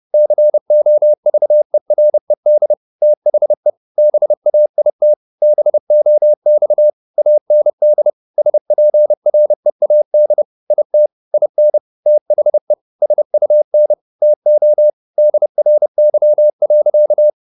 Morse code, heard as a keyed tone, about 30 wpm, 600 Hz. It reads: COVERED THE BAIT BOX AND SPREAD IT IN THE SUN TO DRY.
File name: the old man and the sea / the old man and the sea 1 30wpm